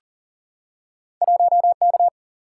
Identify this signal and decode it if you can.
Morse code, heard as a keyed tone, about 38 wpm, 700 Hz. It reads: 1K
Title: Single-Letter Single-Number 40wpm